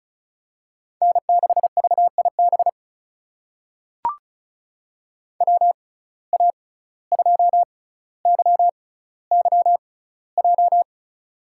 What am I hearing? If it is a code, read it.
Morse code, heard as a keyed tone, about 34 wpm, spaced as for 17 wpm, 700 Hz. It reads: N6VIB WA2YYJ
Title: USA Call Signs - 6x Character Spacing - 35wpm